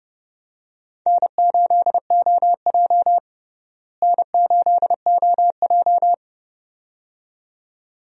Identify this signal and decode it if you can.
Morse code, heard as a keyed tone, about 32 wpm, 700 Hz. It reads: N8OJ N8OJ
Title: USA Call Signs - 4 Characters - Repeated 2x - 30wpm